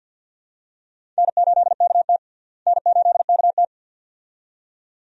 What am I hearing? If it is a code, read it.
Morse code, heard as a keyed tone, about 52 wpm, 700 Hz. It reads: N8KT N8KT
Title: USA Call Signs - 4 Characters - Repeated 2x - 50wpm